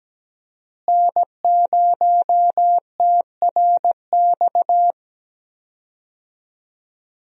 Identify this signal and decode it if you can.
Morse code, heard as a keyed tone, about 17 wpm, 700 Hz. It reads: N0TRX
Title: USA Call Signs - 5 Characters - Repeated 2x - 17wpm